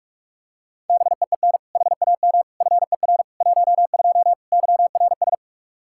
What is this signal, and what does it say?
Morse code, heard as a keyed tone, about 45 wpm, 700 Hz. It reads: BEEN HAM FER 12 YRS
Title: Practice QSO - Exchange 3 - More Info - 45wpm